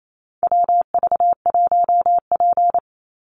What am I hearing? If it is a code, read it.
Morse code, heard as a keyed tone, about 28 wpm, 700 Hz. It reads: WV1P
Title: USA Call Signs 28wpm